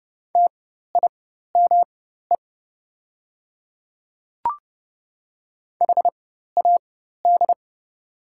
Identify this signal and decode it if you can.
Morse code, heard as a keyed tone, about 29 wpm, spaced as for 14 wpm, 700 Hz. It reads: TIME HAD
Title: 4x Character Spacing - Top 100 Words - 30wpm